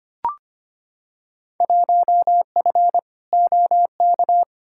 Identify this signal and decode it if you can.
Morse code, heard as a keyed tone, about 25 wpm, 700 Hz. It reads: 1F OK
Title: Field Day Exchanges Encore - 25wpm